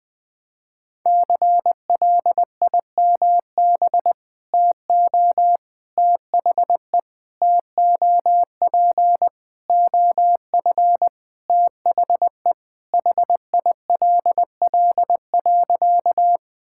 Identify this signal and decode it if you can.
Morse code, heard as a keyed tone, about 20 wpm, 700 Hz. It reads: CLIMB TO THE TOP OF THE HILL.
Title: Sentences with 4-Word Phrases - 20wpm